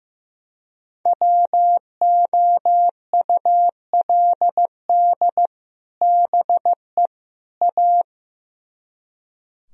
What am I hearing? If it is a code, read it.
Morse code, heard as a keyed tone, about 16 wpm, 700 Hz. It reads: WOULD BE A
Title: Speed-Racing - Sets of 3 Words 15wpm